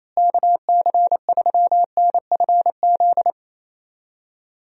Morse code, heard as a keyed tone, about 28 wpm, 700 Hz. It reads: KC3NFZ
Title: USA Call Signs - 6 Characters - Repeated 3x - 28wpm